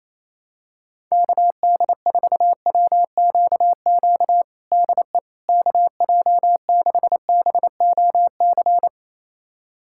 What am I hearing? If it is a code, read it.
Morse code, heard as a keyed tone, about 28 wpm, 700 Hz. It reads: KD4WQQ DE KJ6BOC
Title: USA Callsign DE Callsign - 6 Characters - 28wpm